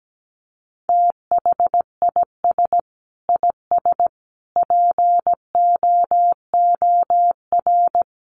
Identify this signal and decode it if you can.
Morse code, heard as a keyed tone, about 17 wpm, 700 Hz. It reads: THIS IS POOR
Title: Sentences from Top 1400 Words 17wpm